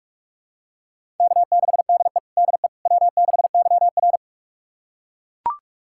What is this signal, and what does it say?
Morse code, heard as a keyed tone, about 42 wpm, 700 Hz. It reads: K6DE DE W6YR
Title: USA Callsign DE Callsign - 4 Characters - 45wpm